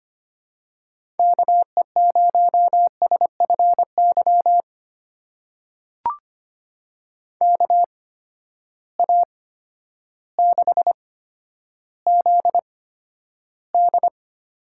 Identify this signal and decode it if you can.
Morse code, heard as a keyed tone, about 25 wpm, spaced as for 11 wpm, 700 Hz. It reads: KE0SFY KA6ZD
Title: USA Call Signs - 6 Characters - 8x Character Spacing - 25wpm